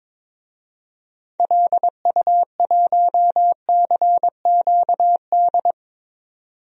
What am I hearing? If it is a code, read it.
Morse code, heard as a keyed tone, about 21 wpm, 700 Hz. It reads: LU1CQD
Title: DXCC-100 Call Signs - 6 Characters - 22wpm